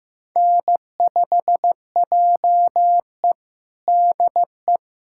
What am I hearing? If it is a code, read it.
Morse code, heard as a keyed tone, about 15 wpm, 700 Hz. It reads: N5JE DE
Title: USA Callsign DE Callsign - 4 Characters - 15wpm